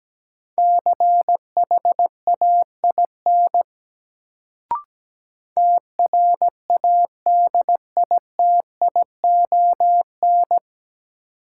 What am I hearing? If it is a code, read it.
Morse code, heard as a keyed tone, about 17 wpm, 700 Hz. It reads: CHAIN TRADITION
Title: Sentences from Top 1500 Words - New Word Drill - 17wpm